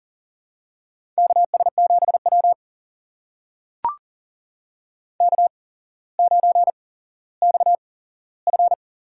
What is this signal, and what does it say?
Morse code, heard as a keyed tone, about 44 wpm, spaced as for 17 wpm, 700 Hz. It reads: KS7W K9XF
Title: USA Call Signs - 4 Characters - 8x Character Spacing - 40wpm